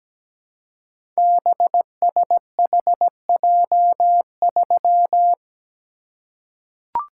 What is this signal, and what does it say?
Morse code, heard as a keyed tone, about 17 wpm, 700 Hz. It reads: BSHJ3
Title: Code Groups - 5 Letter:Numbers - 17wpm